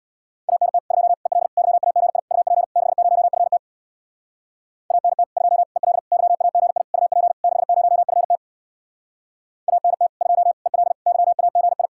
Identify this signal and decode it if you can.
Morse code, heard as a keyed tone, about 60 wpm, 700 Hz. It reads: ANT 2 EL YAGI UP 60FT ANT 2 EL YAGI UP 60FT ANT 2 EL YAGI
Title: Speed-Racing Common QSO Phrases Encore - 50wpm